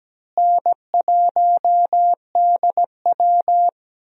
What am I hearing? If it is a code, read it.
Morse code, heard as a keyed tone, about 17 wpm, 700 Hz. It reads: N1DW
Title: USA Call Signs - 4 Characters - Repeated 2x - 17wpm